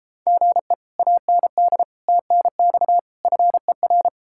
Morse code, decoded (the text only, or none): GE AND TNX FER